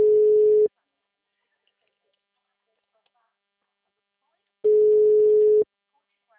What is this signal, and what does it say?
Ringback tone (Europe, 425 Hz), with speech.
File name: out- -2012- - - .39522